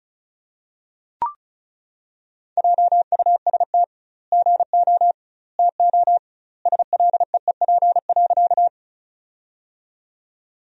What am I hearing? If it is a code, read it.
Morse code, heard as a keyed tone, about 34 wpm, 700 Hz. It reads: JUST GO TO SLEEP.
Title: 2x Word-Spacing - Sentences from Top 500 Words - 35wpm